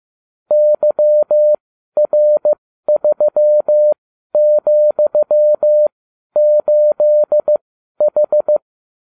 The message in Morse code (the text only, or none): YR3,8H